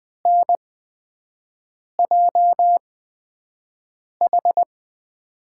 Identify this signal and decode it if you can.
Morse code, heard as a keyed tone, about 20 wpm, spaced as for 5 wpm, 700 Hz. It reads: NJH
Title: USA Call Signs - 5 Characters - 8x Character Spacing - 20wpm